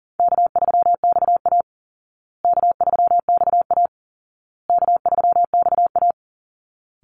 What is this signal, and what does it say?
Morse code, heard as a keyed tone, about 40 wpm, 700 Hz. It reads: K3XA K3XA K3XA
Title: Easy Call Signs Repeated 3x - 40wpm